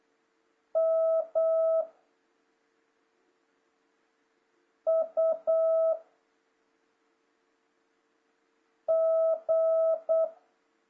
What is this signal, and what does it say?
Morse code, heard as a keyed tone, about 8 wpm, 660 Hz. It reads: M U G
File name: jh8cbh / 2026cw